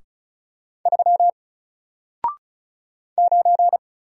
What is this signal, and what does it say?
Morse code, heard as a keyed tone, about 36 wpm, 700 Hz. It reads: 3 9
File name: Single-Letter-Number-35wpm